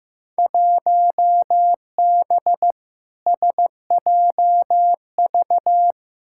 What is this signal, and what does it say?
Morse code, heard as a keyed tone, about 15 wpm, 700 Hz. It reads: 1B SJV
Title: Field Day Exchanges - 15wpm